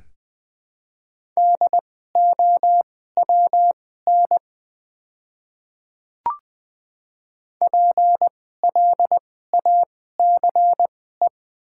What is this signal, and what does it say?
Morse code, heard as a keyed tone, about 20 wpm, spaced as for 14 wpm, 700 Hz. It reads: DOWN PLACE
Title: 2x Character Spacing - Top 200 Words - 20wpm